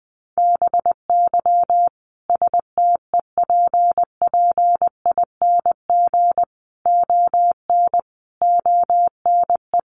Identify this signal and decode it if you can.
Morse code, heard as a keyed tone, about 20 wpm, 700 Hz. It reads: BY STEPPING ON ONE
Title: A Sound Of Thunder 20wpm